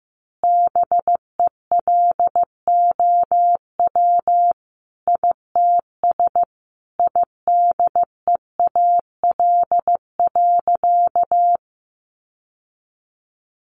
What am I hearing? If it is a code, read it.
Morse code, heard as a keyed tone, about 14 wpm, 700 Hz. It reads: BELOW ITS IDEAL.
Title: Sentences from Top 1400 Words 15wpm